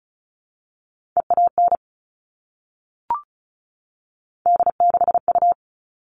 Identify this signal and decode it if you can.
Morse code, heard as a keyed tone, about 34 wpm, 700 Hz. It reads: EAN D6U
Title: All Cut Numbers - 3-Digit 35wpm